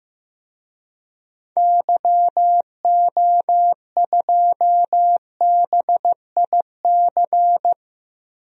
Morse code, heard as a keyed tone, about 15 wpm, 700 Hz. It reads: YO2BIC
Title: DXCC-100 Call Signs - 6 Characters - Repeated 3x - 15wpm